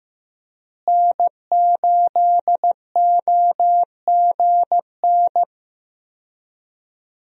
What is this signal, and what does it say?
Morse code, heard as a keyed tone, about 15 wpm, 700 Hz. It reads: N8OGN
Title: USA Call Signs - 5 Characters - Repeated 2x - 15wpm